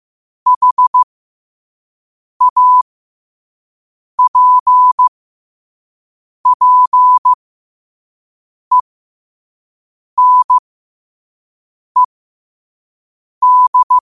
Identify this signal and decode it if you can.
Morse code, heard as a keyed tone, about 15 wpm, spaced as for 5 wpm, 1000 Hz. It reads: HAPPENED